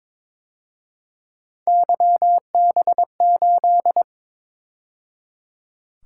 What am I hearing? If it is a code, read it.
Morse code, heard as a keyed tone, about 26 wpm, 700 Hz. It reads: YB8
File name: Speed-Racing - Three Letter-Number Combinations - 22wpm